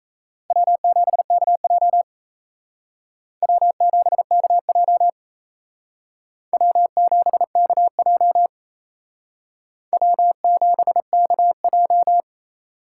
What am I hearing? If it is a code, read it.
Morse code, heard as a keyed tone, about 35 wpm, 700 Hz. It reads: W7KJ W7KJ W7KJ W7KJ
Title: Speed Racing - USA 1x2 Call Signs 28wpm